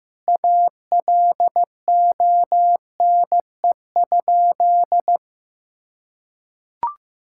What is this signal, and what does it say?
Morse code, heard as a keyed tone, about 15 wpm, 700 Hz. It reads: ALONE?
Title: Sentences from Top 1600 Words 15wpm